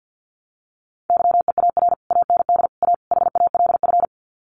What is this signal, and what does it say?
Morse code, heard as a keyed tone, about 50 wpm, 700 Hz. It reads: YEAR AND A HALF
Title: Sets of 4 Words 50wpm